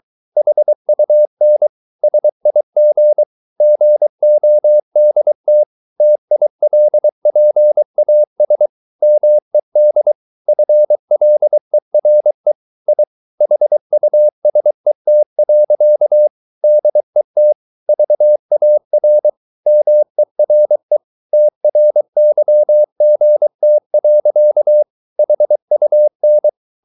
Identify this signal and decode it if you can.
Morse code, heard as a keyed tone, about 23 wpm, 600 Hz. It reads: HUN SIG GODT TILPAS MED FLERE I HUSET. DET VAR MERE TRYGT. HUN
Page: Ferie 23 wpm